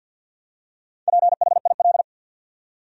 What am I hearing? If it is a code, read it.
Morse code, heard as a keyed tone, about 53 wpm, 700 Hz. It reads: PHIL
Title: USA Common Names - 50wpm